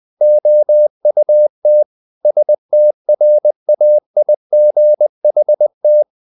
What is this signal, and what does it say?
Morse code, heard as a keyed tone, about 20 wpm, 600 Hz. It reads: OUT STRAIGHT
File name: the old man and the sea part 2 20wpm